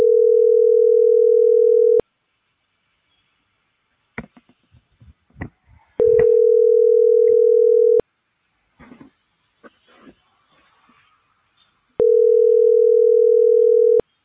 Ringback tone (North America), 440 + 480 Hz.